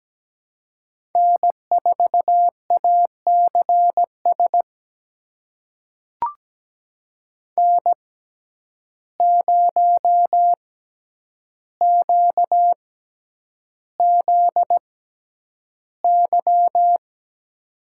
Morse code, heard as a keyed tone, about 17 wpm, spaced as for 9 wpm, 700 Hz. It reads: N4ACS N0QZY
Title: USA Call Signs - 5 Characters - 6x Character Spacing - 17wpm